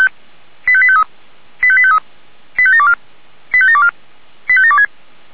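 Five-tone selective call (ZVEI), heard as quick runs of five tones, tones of about 70 ms. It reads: 85853 85853 85724 85724 85726
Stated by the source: ZVEI five-tone selective call received on an SDR.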